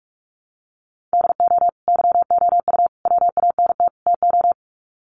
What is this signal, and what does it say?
Morse code, heard as a keyed tone, about 44 wpm, 700 Hz. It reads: DO YOU WANT TO
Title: Sets of 4 Words 45wpm